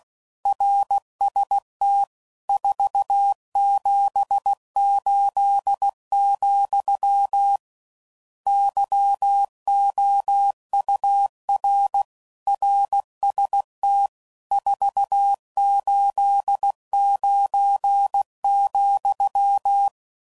Morse code, heard as a keyed tone, about 16 wpm, 800 Hz. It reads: RST 478, YOUR RST 489,
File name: CW Code Groups @15WPM / RSTs / Your RSTs